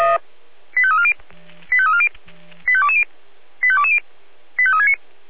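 Five-tone selective call (ZVEI), heard as quick runs of five tones, tones of about 70 ms. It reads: 85340 85340 85220 85220 85369
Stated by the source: ZVEI five-tone selective call received by SDR.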